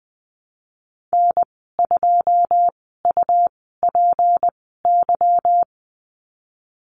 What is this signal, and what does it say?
Morse code, heard as a keyed tone, about 20 wpm, spaced as for 15 wpm, 700 Hz. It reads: N2UPY
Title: 154 Lesson 25g - Letter p - Call Signs 20wpm